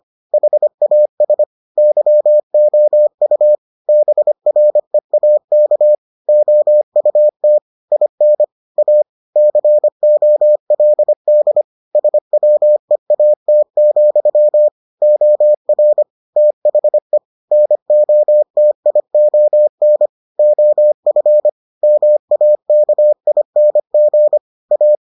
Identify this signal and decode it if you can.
Morse code, heard as a keyed tone, about 25 wpm, 600 Hz. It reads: HAS YOU BREAK OUT IN A COLD SWEAT, OR THE NOTION OF MAKING A